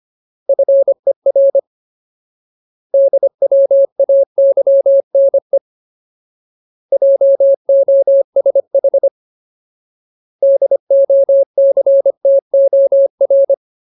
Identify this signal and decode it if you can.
Morse code, heard as a keyed tone, about 25 wpm, 550 Hz. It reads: FER DWAYNE JOSH DOCTOR
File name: Ultimate Word List / 25WPM Ultimate